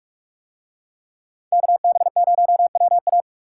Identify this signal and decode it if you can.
Morse code, heard as a keyed tone, about 41 wpm, 700 Hz. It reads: KB0WA